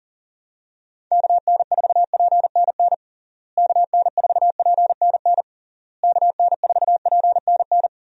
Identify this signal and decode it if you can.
Morse code, heard as a keyed tone, about 37 wpm, 700 Hz. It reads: KN4PNN KN4PNN KN4PNN
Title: USA Call Signs - 6 Characters - Repeated 3x - 40wpm